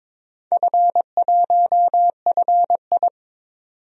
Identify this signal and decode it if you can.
Morse code, heard as a keyed tone, about 23 wpm, 700 Hz. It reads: F1FI
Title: DXCC-100 Call Signs - 4 Characters - 22wpm